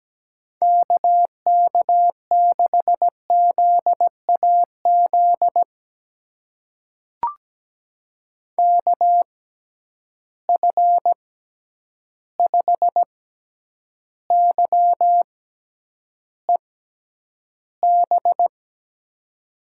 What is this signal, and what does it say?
Morse code, heard as a keyed tone, about 17 wpm, spaced as for 9 wpm, 700 Hz. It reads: KK6ZAZ KF5YEB
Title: USA Call Signs - 6 Characters - 6x Character Spacing - 17wpm